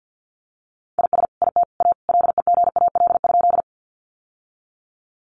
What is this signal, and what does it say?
Morse code, heard as a keyed tone, about 50 wpm, 700 Hz. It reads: IS IT A LEGAL?
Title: Sentences from Top 1200 Words 50wpm